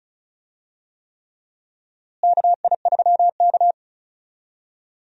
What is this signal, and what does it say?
Morse code, heard as a keyed tone, about 36 wpm, 700 Hz. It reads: KI3K
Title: USA Call Signs - 4 Characters - Repeated 3x - 35wpm